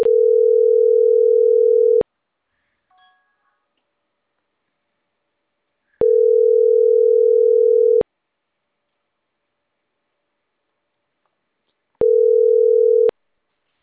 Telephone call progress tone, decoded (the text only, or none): ringback tone